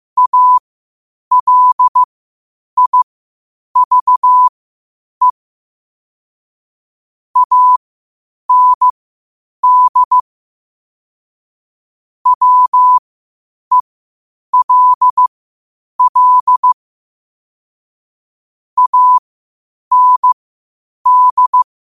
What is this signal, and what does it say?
Morse code, heard as a keyed tone, about 15 wpm, spaced as for 8 wpm, 1000 Hz. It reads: ALIVE AND WELL AND
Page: Quotes for Thu, 14 Aug 2025 in Morse Code at 8 words per minute.